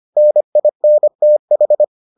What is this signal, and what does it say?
Morse code, heard as a keyed tone, about 25 wpm, 600 Hz. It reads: NINTH